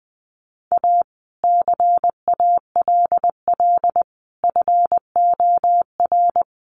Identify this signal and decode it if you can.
Morse code, heard as a keyed tone, about 20 wpm, 700 Hz. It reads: A CALL FOR
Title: Sentences from Top 300 Words 20wpm